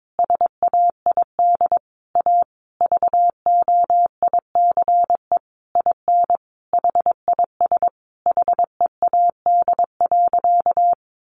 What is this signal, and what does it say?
Morse code, heard as a keyed tone, about 22 wpm, 700 Hz. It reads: SAID A VOICE IN HIS HEAD.
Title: Sentences from Top 1300 Words 22wpm